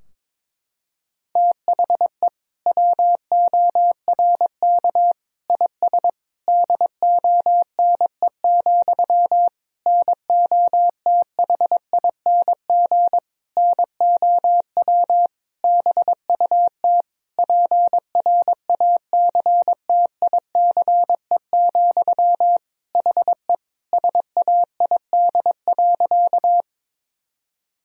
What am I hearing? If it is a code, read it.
Morse code, heard as a keyed tone, about 22 wpm, 700 Hz. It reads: THE WORK IS DONE, NOTHING NOW BUT PRACTICE, HE SAID.
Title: Sentences from Top 1000 Words - Review Part 4 - 22wpm